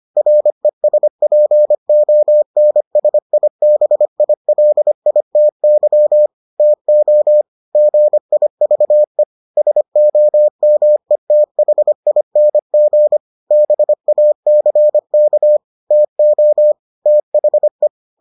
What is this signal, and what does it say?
Morse code, heard as a keyed tone, about 25 wpm, 600 Hz. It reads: RESPONSIBILITY TO GIVE SOMETHING BACK TO THE